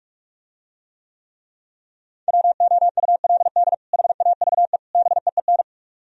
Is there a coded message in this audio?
Yes: Morse code from 2.3 to 5.6 s, heard as a keyed tone, about 46 wpm, 700 Hz. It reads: WOULD HAVE BEEN